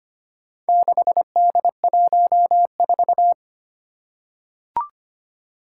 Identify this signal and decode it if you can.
Morse code, heard as a keyed tone, about 25 wpm, 700 Hz. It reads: 6D14
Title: Four Challenging Letter-Number Combinations 25wpm